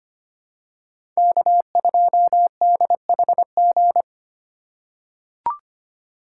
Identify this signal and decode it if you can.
Morse code, heard as a keyed tone, about 25 wpm, 700 Hz. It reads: K2DHG
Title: USA Call Signs - 5 Characters - 3x Character Spacing - 25wpm